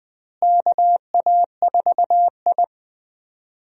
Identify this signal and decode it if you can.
Morse code, heard as a keyed tone, about 21 wpm, 700 Hz. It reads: KA4I
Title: USA Call Signs - 4 Characters - Repeated 3x - 20wpm